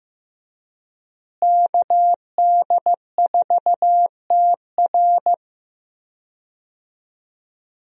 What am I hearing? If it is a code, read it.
Morse code, heard as a keyed tone, about 15 wpm, 700 Hz. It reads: KD4TR
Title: USA Call Signs - 5 Characters - Repeated 3x - 15wpm